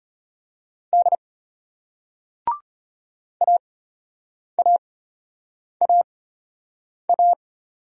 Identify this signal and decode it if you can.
Morse code, heard as a keyed tone, about 34 wpm, spaced as for 8 wpm, 700 Hz. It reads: D AAAA